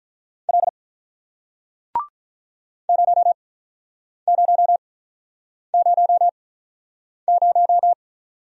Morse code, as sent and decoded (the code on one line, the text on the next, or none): .-.. / ----- ----- ----- -----
L 0000